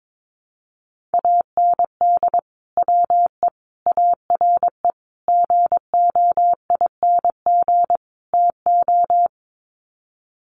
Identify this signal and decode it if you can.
Morse code, heard as a keyed tone, about 22 wpm, 700 Hz. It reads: AND WE ARE GOING TO
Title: Sets of 5 Words Encore2 22wpm